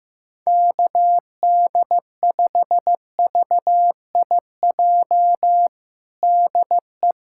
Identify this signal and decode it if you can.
Morse code, heard as a keyed tone, about 15 wpm, 700 Hz. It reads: KD5VIJ DE
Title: USA Callsign DE Callsign - 6 Characters - 15wpm